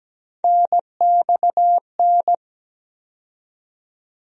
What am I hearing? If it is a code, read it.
Morse code, heard as a keyed tone, about 17 wpm, 700 Hz. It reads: NXN
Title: USA Call Sign Postfixes - 3 Letters 17wpm